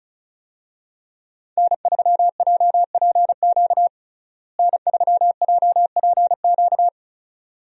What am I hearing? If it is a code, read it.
Morse code, heard as a keyed tone, about 34 wpm, 700 Hz. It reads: N3JPQ N3JPQ
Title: USA Call Signs - 5 Characters - Repeated 2x - 35wpm